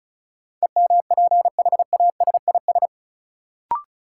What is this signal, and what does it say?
Morse code, heard as a keyed tone, about 36 wpm, 700 Hz. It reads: EMPHASIS
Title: Sentences from Top 1400 Words - New Word Drill - 35wpm